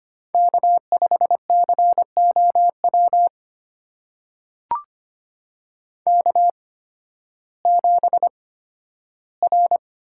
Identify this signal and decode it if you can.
Morse code, heard as a keyed tone, about 25 wpm, spaced as for 13 wpm, 700 Hz. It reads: K5COW K7R
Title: USA Call Signs - 5 Characters - 8x Character Spacing - 25wpm